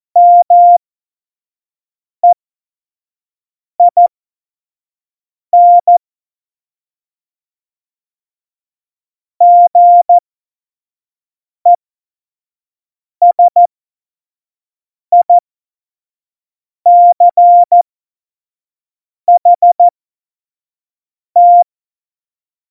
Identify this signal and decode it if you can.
Morse code, heard as a keyed tone, about 14 wpm, spaced as for 5 wpm, 700 Hz. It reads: MEIN GESICHT
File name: Morsekurs / Witze / Witz 04. Juni 2019, 25 F70